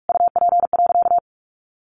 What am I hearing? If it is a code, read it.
Morse code, heard as a keyed tone, about 44 wpm, 700 Hz. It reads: UP.
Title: A Sound Of Thunder 45wpm